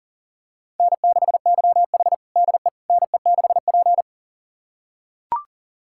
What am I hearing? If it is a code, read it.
Morse code, heard as a keyed tone, about 40 wpm, 700 Hz. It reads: N6YH DE NE6P
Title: USA Callsign DE Callsign - 4 Characters - 40wpm